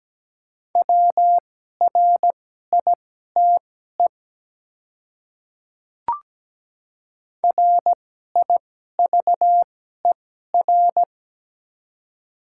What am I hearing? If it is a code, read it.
Morse code, heard as a keyed tone, about 17 wpm, spaced as for 12 wpm, 690 Hz. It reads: WRITE RIVER
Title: 2x Character Spacing - Top 300 Words - 17wpm